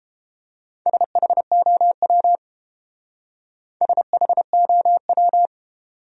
Morse code, heard as a keyed tone, about 28 wpm, 700 Hz. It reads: SHOW SHOW